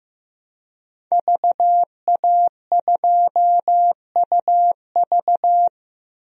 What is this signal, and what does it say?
Morse code, heard as a keyed tone, about 16 wpm, 700 Hz. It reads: VA2UV
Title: DXCC-100 Call Signs - 5 Characters - Repeated 2x - 15wpm